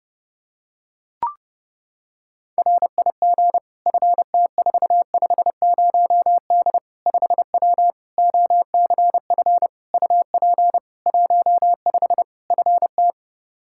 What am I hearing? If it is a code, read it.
Morse code, heard as a keyed tone, about 32 wpm, 700 Hz. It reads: RIG FT450D 5W OCF UP 15 FT
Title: K1USN SST QSO Practice 30wpm